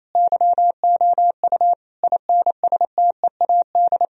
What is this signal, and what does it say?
Morse code, heard as a keyed tone, about 28 wpm, 700 Hz. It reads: YOU INSTEAD
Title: Sentences from Top 1000 Words - Review Part 4 - 28wpm